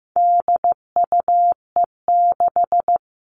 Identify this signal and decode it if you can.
Morse code, heard as a keyed tone, about 15 wpm, 700 Hz. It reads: DUE6
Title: All Cut Numbers - 4-Digit 15wpm